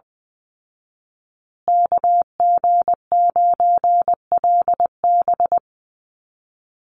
Morse code, heard as a keyed tone, about 20 wpm, 700 Hz. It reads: KG9LB
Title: Call Signs Repeated 2x - 20wpm